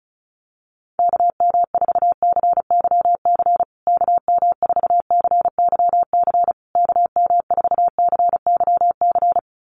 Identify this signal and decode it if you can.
Morse code, heard as a keyed tone, about 34 wpm, 700 Hz. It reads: KM4CYC KM4CYC KM4CYC
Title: USA Call Signs Repeated 3x 35wpm